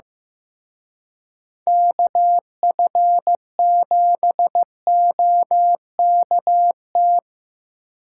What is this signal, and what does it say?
Morse code, heard as a keyed tone, about 16 wpm, 700 Hz. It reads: KF7OKT
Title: Speed Racing - USA Call Signs 15wpm